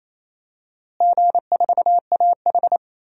Morse code, heard as a keyed tone, about 28 wpm, 700 Hz. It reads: G4AH
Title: DXCC-100 Call Signs - 4 Characters - Repeated 3x - 28wpm